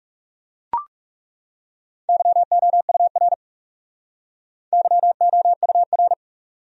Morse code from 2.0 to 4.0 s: YOUR

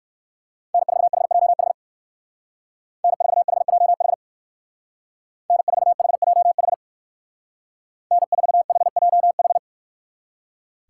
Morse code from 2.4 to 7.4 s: N4HJH N4HJH

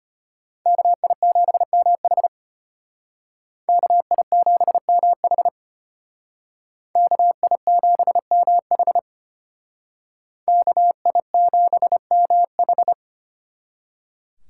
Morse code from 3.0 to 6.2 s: KI7MH

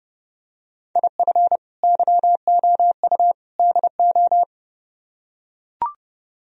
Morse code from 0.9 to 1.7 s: IF

